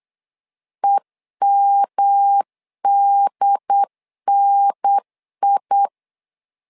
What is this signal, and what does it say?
Morse code, heard as a keyed tone, about 8 wpm, 800 Hz. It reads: EMDNI
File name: LEZIONI / 3 MWRO / 3 M 8wpm 2300